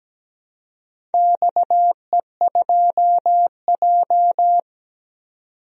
Morse code, heard as a keyed tone, about 17 wpm, 700 Hz. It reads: XE2J
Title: DXCC-100 Call Signs - 4 Characters - 17wpm